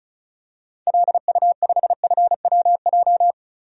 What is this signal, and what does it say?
Morse code, heard as a keyed tone, about 36 wpm, 700 Hz. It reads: LU5FWJ
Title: DXCC-100 Call Signs - 6 Characters - 35wpm